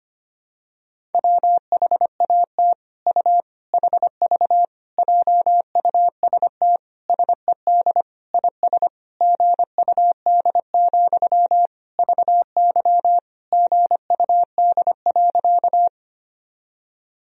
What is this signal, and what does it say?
Morse code, heard as a keyed tone, about 25 wpm, 700 Hz. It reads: WHAT U HV JUST SED IS GUD, VY GUD.
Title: Sentences from Top 200 Words and Abbreviations 25wpm